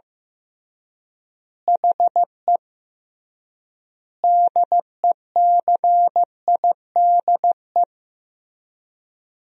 Morse code, heard as a keyed tone, about 15 wpm, 700 Hz. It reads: HE DECIDE
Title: 3x Word-Spacing - Sentences from Top 500 Words - 15wpm